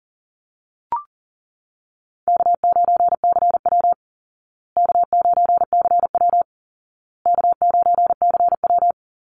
Morse code, heard as a keyed tone, about 40 wpm, 700 Hz. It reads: K9CW K9CW K9CW
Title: Call Signs Repeated 2x - 40wpm